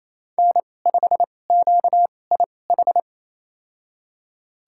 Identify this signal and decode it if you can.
Morse code, heard as a keyed tone, about 28 wpm, spaced as for 20 wpm, 700 Hz. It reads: N5QIH
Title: USA Call Signs - 2x Character Spacing - 28wpm